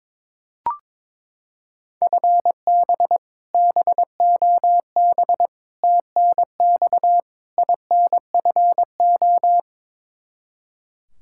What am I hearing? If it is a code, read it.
Morse code, heard as a keyed tone, about 22 wpm, 700 Hz. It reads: FB BOB TNX INFO
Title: Practice QSO - Exchange 3 - More Info - 22wpm